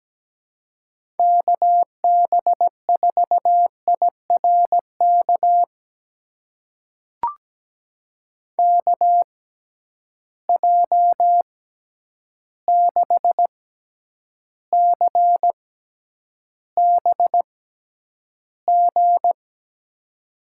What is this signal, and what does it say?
Morse code, heard as a keyed tone, about 17 wpm, spaced as for 9 wpm, 700 Hz. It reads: KB4IRK KJ6CBG
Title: USA Call Signs - 6 Characters - 6x Character Spacing - 17wpm